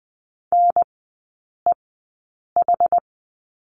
Morse code, heard as a keyed tone, about 20 wpm, spaced as for 8 wpm, 700 Hz. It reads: NEH
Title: 081 Lesson 14e - Number 2 - Three Characters 20wpm